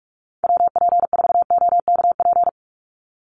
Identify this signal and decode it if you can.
Morse code, heard as a keyed tone, about 44 wpm, 700 Hz. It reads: WP4OKP
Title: USA Call Signs 45wpm